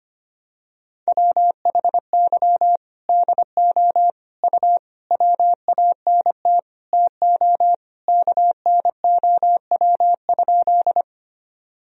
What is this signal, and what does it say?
Morse code, heard as a keyed tone, about 25 wpm, 700 Hz. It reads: WHY DO U WANT TO KNOW?
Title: Sentences from Top 200 Words and Abbreviations 25wpm